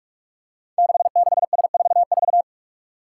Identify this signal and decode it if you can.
Morse code, heard as a keyed tone, about 44 wpm, 700 Hz. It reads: 66S44
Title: Five Challenging Letter-Number Combinations 45wpm